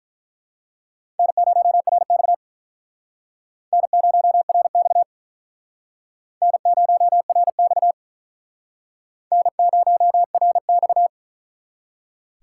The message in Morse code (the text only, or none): N0RX N0RX N0RX N0RX